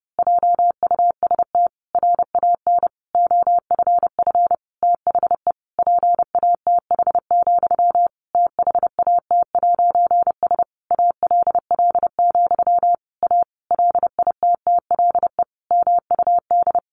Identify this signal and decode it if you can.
Morse code, heard as a keyed tone, about 30 wpm, 700 Hz. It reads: JUST RAN OFF THE PATH, THAT'S ALL, A LITTLE MUD
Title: A Sound Of Thunder 30wpm